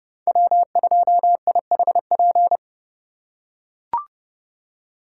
Morse code, heard as a keyed tone, about 30 wpm, 700 Hz. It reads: W2IHP
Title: USA Call Signs - 5 Characters - 4x Character Spacing - 30wpm